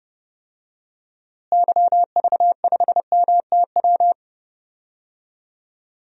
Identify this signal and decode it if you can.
Morse code, heard as a keyed tone, about 32 wpm, 700 Hz. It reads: YV5MTW
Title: DXCC-100 Call Signs - 6 Characters - Repeated 2x - 30wpm